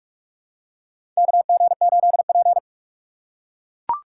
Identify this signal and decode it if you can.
Morse code, heard as a keyed tone, about 43 wpm, 700 Hz. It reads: KG8P
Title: USA Call Signs - 3x Character Spacing - 45wpm